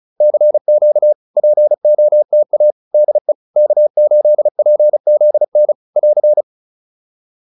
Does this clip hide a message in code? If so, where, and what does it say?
Yes: Morse code from 0.2 to 6.4 s, heard as a keyed tone, about 37 wpm, 600 Hz. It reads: CQ POTA DE K8PZN <AR>